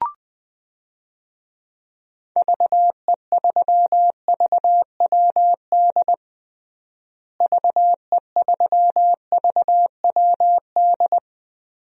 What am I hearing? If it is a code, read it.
Morse code, heard as a keyed tone, about 21 wpm, 700 Hz. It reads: VE3VWD VE3VWD
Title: DXCC-100 Call Signs - 6 Characters - Repeated 3x - 20wpm